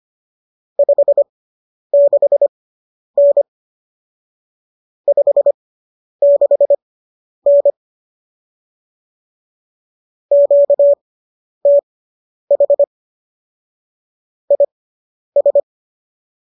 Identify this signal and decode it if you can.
Morse code, heard as a keyed tone, about 25 wpm, spaced as for 10 wpm, 570 Hz. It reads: 56N 56N QTH IS